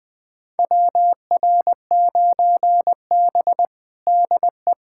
Morse code, heard as a keyed tone, about 21 wpm, 700 Hz. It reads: WR9B DE
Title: USA Callsign DE Callsign - 4 Characters - 20wpm